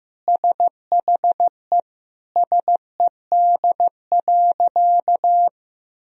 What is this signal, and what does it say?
Morse code, heard as a keyed tone, about 15 wpm, 700 Hz. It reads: SHE SED.
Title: Sentences from Top 300 Words and Abbreviations 15wpm